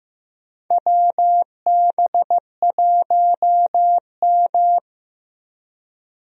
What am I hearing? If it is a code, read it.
Morse code, heard as a keyed tone, about 15 wpm, 700 Hz. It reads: WB1M
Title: USA Call Signs - 4 Characters - Repeated 2x - 15wpm